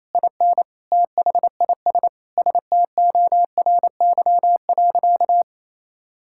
Morse code, heard as a keyed tone, about 28 wpm, 700 Hz. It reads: IN THIS STORY.
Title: Sentences from Top 1600 Words 28wpm